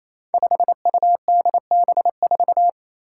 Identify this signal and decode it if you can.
Morse code, heard as a keyed tone, about 27 wpm, 700 Hz. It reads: 5UDB4
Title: Five Challenging Letter-Number Combinations 28wpm